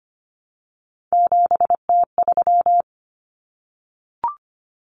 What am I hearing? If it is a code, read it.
Morse code, heard as a keyed tone, about 25 wpm, 700 Hz. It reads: 7T3
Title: Common Cut Numbers - 3-Digit 25wpm